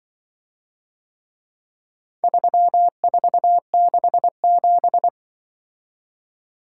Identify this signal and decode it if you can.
Morse code, heard as a keyed tone, about 24 wpm, 700 Hz. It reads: 3467
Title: Speed-Racing - Four Letter-Number Combinations - 20wpm